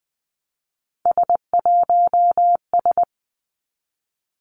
Morse code, heard as a keyed tone, about 20 wpm, 700 Hz. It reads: S1S
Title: Three Challenging Letter-Number Combinations 20wpm